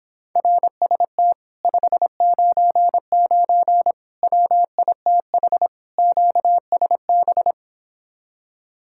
Morse code, heard as a keyed tone, about 26 wpm, 700 Hz. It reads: RST 599 WITH QSB